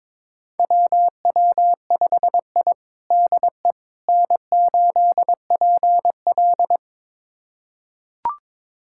Morse code, heard as a keyed tone, about 22 wpm, 700 Hz. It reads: WW5I DE N8PL
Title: USA Callsign DE Callsign - 4 Characters - 22wpm